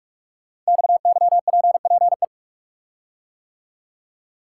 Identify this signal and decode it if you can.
Morse code, heard as a keyed tone, about 44 wpm, 690 Hz. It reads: XYPPE
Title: Code Groups - 5 Letters - 45wpm